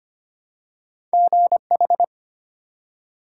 Morse code, heard as a keyed tone, about 25 wpm, 700 Hz. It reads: GH
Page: morse-code-ninja-practice-25wpm